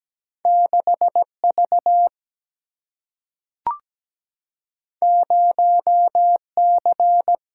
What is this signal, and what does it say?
Morse code, heard as a keyed tone, about 17 wpm, 700 Hz. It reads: 6 V 0 C
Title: Single-Letter Single-Number 17wpm